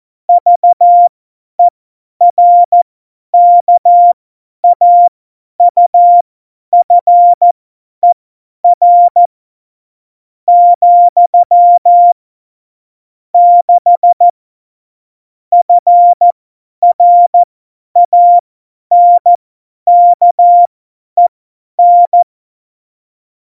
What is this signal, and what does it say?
Morse code, heard as a keyed tone, about 14 wpm, spaced as for 10 wpm, 700 Hz. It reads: VERKAUFER , 6 FRANKEN
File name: Morsekurs / Witze / Witz 16. Oktober 2018, 50 F70